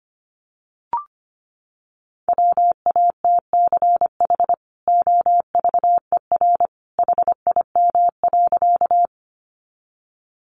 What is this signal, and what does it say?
Morse code, heard as a keyed tone, about 25 wpm, 700 Hz. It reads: WATCH OVER HIM.
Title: Sentences from Top 300 Words 25wpm